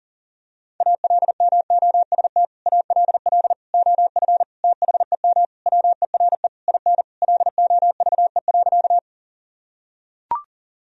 Morse code, heard as a keyed tone, about 40 wpm, 700 Hz. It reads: ALMOST ALL OF THEM WERE IN LOVE.
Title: Sentences from Top 1000 Words - Review Part 4 - 40wpm